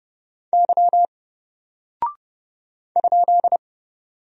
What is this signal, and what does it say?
Morse code, heard as a keyed tone, about 30 wpm, 700 Hz. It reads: Y ?
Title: Single Character (Letter:Number:Punctuation:Prosign) Alt 30wpm